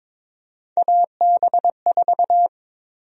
Morse code, heard as a keyed tone, about 22 wpm, 700 Hz. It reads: AB4
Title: USA Call Sign Prefixes - Repeated 2x - 22wpm